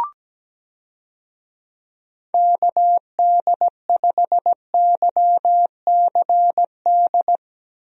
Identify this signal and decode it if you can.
Morse code, heard as a keyed tone, about 17 wpm, 700 Hz. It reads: KD5YCD
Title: USA Call Signs - 6 Characters - Repeated 2x - 17wpm